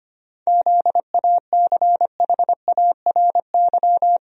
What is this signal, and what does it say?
Morse code, heard as a keyed tone, about 25 wpm, 700 Hz. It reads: ZACHARY
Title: USA Common Names - 25wpm